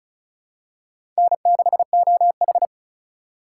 Morse code, heard as a keyed tone, about 36 wpm, 690 Hz. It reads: N6OH
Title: USA Call Signs - 4 Characters - 2x Character Spacing - 35wpm